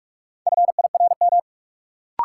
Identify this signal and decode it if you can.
Morse code, heard as a keyed tone, about 46 wpm, 700 Hz. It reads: FIRM
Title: Sentences from Top 1400 Words - New Word Drill - 45wpm